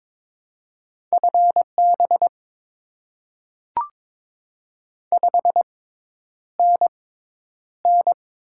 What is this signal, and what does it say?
Morse code, heard as a keyed tone, about 22 wpm, spaced as for 10 wpm, 690 Hz. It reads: FB 5NN
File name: Top 20 QSO Elements - 6x Character Spacing - 22wpm